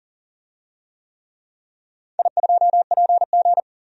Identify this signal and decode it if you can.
Morse code, heard as a keyed tone, about 40 wpm, 700 Hz. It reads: I2PG
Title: DXCC-100 Call Signs - 4 Characters - 40wpm